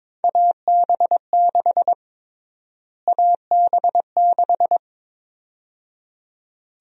Morse code, heard as a keyed tone, about 22 wpm, 700 Hz. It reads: AB6 AB6
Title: USA Call Sign Prefixes - Repeated 2x - 22wpm